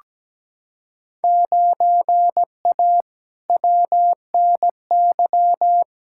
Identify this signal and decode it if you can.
Morse code, heard as a keyed tone, about 17 wpm, 700 Hz. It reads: 9A WNY
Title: Field Day Exchanges Encore - 17wpm